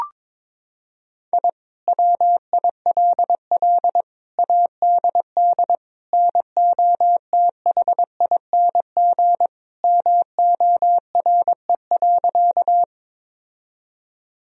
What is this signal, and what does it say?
Morse code, heard as a keyed tone, about 22 wpm, 700 Hz. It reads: I WILL ADD NOTHING MORE.
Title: Sentences from Top 1000 Words - Review Part 1 - 22wpm